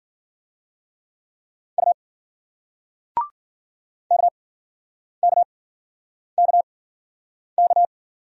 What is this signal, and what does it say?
Morse code, heard as a keyed tone, about 51 wpm, spaced as for 9 wpm, 700 Hz. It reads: U KKKK